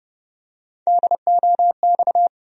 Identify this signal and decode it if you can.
Morse code, heard as a keyed tone, about 30 wpm, 700 Hz. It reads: DOX
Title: USA Call Sign Postfixes - 3 Letters 30wpm